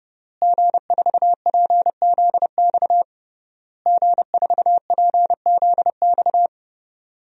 Morse code, heard as a keyed tone, about 31 wpm, 700 Hz. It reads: G4PZX G4PZX
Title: DXCC-100 Call Signs - 5 Characters - Repeated 3x - 30wpm